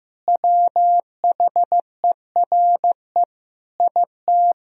Morse code, heard as a keyed tone, about 15 wpm, 700 Hz. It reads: WHERE IT
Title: Sentences from Top 1000 Words - Review Part 1 - 15wpm